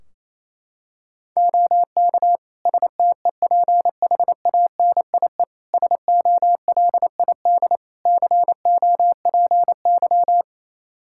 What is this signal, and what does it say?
Morse code, heard as a keyed tone, about 28 wpm, 700 Hz. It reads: OK STEPHANIE SOLID COPY
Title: Practice QSO - Exchange 3 - More Info - 28wpm